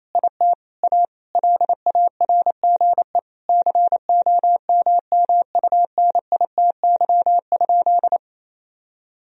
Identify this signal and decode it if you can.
Morse code, heard as a keyed tone, about 28 wpm, 700 Hz. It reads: IT A LARGE COMMUNITY?
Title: Sentences from Top 1000 Words - Review Part 4 - 28wpm